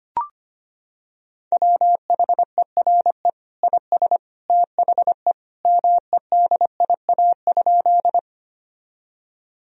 Morse code, heard as a keyed tone, about 25 wpm, 700 Hz. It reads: WHERE IS THE MEDIA?
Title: Sentences from Top 1000 Words - Review Part 4 - 25wpm